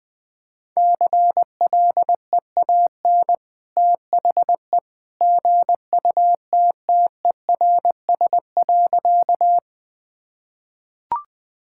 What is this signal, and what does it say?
Morse code, heard as a keyed tone, about 20 wpm, 700 Hz. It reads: CLEAN THE GUTTERS.
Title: Easy Sentences with Hints - 20wpm